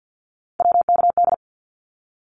Morse code, heard as a keyed tone, about 50 wpm, 700 Hz. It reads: WKD
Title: Abbreviations Repeated 3x - 50wpm